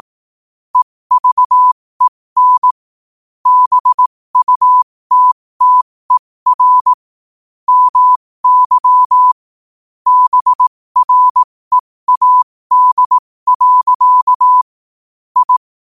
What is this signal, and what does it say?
Morse code, heard as a keyed tone, about 18 wpm, spaced as for 15 wpm, 1000 Hz. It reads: EVEN BUTTER MY BREAD. I
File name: qotdfiles / 15wpm